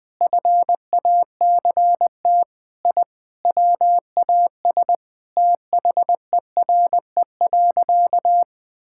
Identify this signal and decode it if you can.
Morse code, heard as a keyed tone, about 20 wpm, 700 Hz. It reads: FACT I WAS THERE.
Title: Sentences with 5-Word Phrases - 20wpm